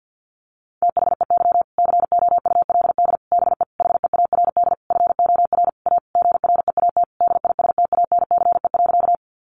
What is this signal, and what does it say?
Morse code, heard as a keyed tone, about 50 wpm, 700 Hz. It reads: THEY COULD BE HEARD FOR A GREAT DISTANCE.
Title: Sentences from Top 1200 Words 50wpm